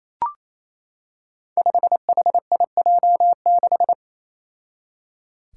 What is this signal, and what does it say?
Morse code, heard as a keyed tone, about 28 wpm, 700 Hz. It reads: 5HIJ6
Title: Five Challenging Letter-Number Combinations 28wpm